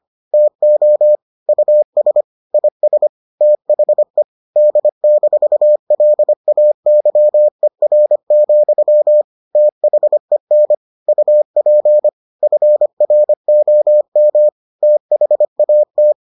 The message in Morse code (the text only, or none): TO US IS THE D-LAYER, THEN UP FROM THAT